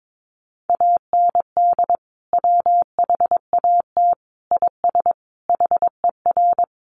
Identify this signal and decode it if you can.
Morse code, heard as a keyed tone, about 22 wpm, 700 Hz. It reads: AND WHAT IS HER
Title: Sentences from Top 200 Words 22wpm